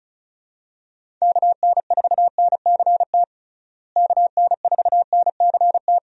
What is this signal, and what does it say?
Morse code, heard as a keyed tone, about 36 wpm, 700 Hz. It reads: KN4NCT KN4NCT
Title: USA Call Signs - 6 Characters - Repeated 2x - 35wpm